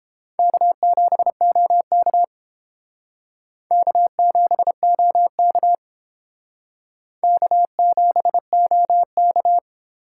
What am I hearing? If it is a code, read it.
Morse code, heard as a keyed tone, about 30 wpm, 700 Hz. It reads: K7OK K7OK K7OK